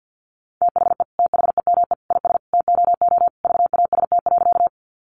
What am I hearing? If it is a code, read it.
Morse code, heard as a keyed tone, about 50 wpm, 700 Hz. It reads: THE THEME IS TOO VAST.
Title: Sentences from Top 1300 Words 50wpm